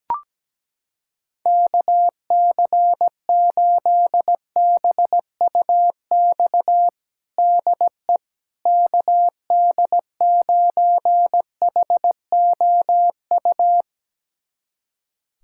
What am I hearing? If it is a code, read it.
Morse code, heard as a keyed tone, about 17 wpm, 700 Hz. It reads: KC8BUX DE KD9HOU
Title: USA Callsign DE Callsign - 6 Characters - 17wpm